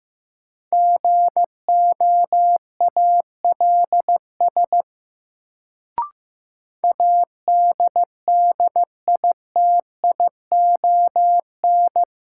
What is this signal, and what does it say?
Morse code, heard as a keyed tone, about 15 wpm, 700 Hz. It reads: GOALS ADDITION
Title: Sentences from Top 1000 Words - Review Part 6 - New Word Drill - 15wpm